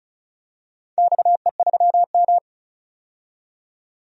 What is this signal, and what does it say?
Morse code, heard as a keyed tone, about 36 wpm, 700 Hz. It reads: XE3M
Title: DXCC-100 Call Signs - 4 Characters - 35wpm